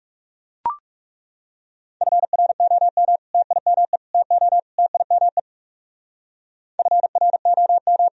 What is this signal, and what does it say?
Morse code, heard as a keyed tone, about 37 wpm, 700 Hz. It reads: FROM TIME TO TIME FROM
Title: Speed-Racing - Sets of 4 Words 30wpm